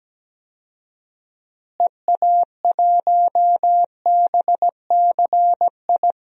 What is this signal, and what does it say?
Morse code, heard as a keyed tone, about 17 wpm, 700 Hz. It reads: EA1BCI
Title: DXCC-100 Call Signs - 6 Characters - Repeated 3x - 17wpm